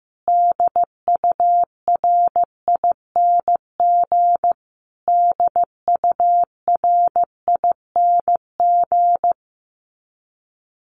Morse code, heard as a keyed tone, about 15 wpm, 700 Hz. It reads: DURING DURING
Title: Top 500 Words Repeated 15wpm